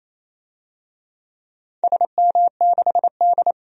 Morse code, heard as a keyed tone, about 28 wpm, 700 Hz. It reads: SM6D
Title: DXCC-100 Call Signs - 4 Characters - Repeated 2x - 28wpm